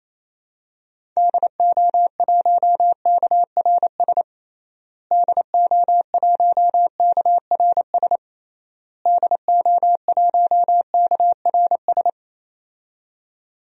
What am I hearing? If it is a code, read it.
Morse code, heard as a keyed tone, about 28 wpm, 700 Hz. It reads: DO1KRS DO1KRS DO1KRS
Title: DXCC-100 Call Signs - 6 Characters - Repeated 3x - 28wpm